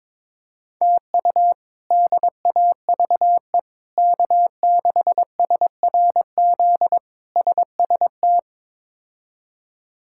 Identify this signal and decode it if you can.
Morse code, heard as a keyed tone, about 22 wpm, 700 Hz. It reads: TU DAVE K6SRZ SST
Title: K1USN SST QSO Practice - 22wpm